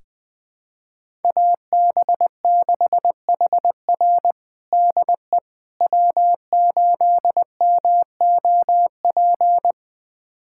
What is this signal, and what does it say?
Morse code, heard as a keyed tone, about 20 wpm, 700 Hz. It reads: AB6HR DE W8MOP
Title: USA Callsign DE Callsign - 5 Characters - 20wpm